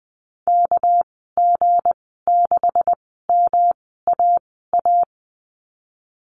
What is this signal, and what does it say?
Morse code, heard as a keyed tone, about 19 wpm, spaced as for 14 wpm, 700 Hz. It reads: KG6MAA
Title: 219 Lesson 35g - Number 0 - Call Signs 20wpm